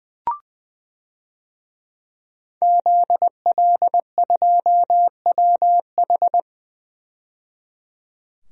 Morse code, heard as a keyed tone, about 20 wpm, 700 Hz. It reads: ZL2WH
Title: DXCC-100 Call Signs - 5 Characters - 20wpm